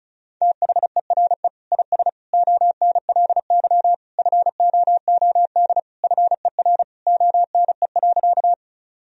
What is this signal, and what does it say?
Morse code, heard as a keyed tone, about 36 wpm, 700 Hz. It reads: THERE IS ONLY FOOD FER ONE.
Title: Sentences from Top 300 Words and Abbreviations 35wpm